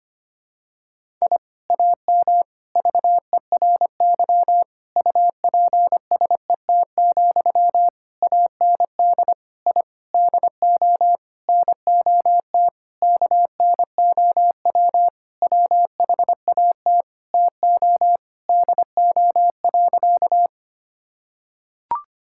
Morse code, heard as a keyed tone, about 25 wpm, 700 Hz. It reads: I AM VERY UPSET, AND I DO NOT KNOW WHAT TO DO.
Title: Sentences from Top 1000 Words - Review Part 5 - 25wpm